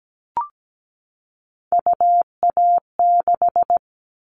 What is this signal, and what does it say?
Morse code, heard as a keyed tone, about 17 wpm, 700 Hz. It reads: UA6
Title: Callsign Prefixes for Top 25 DXCC - 17wpm